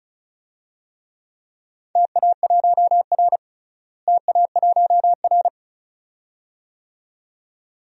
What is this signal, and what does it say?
Morse code, heard as a keyed tone, about 36 wpm, 700 Hz. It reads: TA1R TA1R
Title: DXCC-100 Call Signs - 4 Characters - Repeated 2x - 35wpm